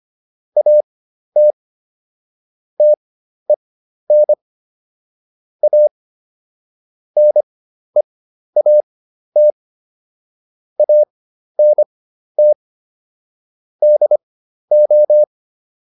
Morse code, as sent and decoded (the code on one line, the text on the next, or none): .- - / - . -. / .- / -. .- - / .- -. - / -.. ---
AT TEN A NEAT ANT DO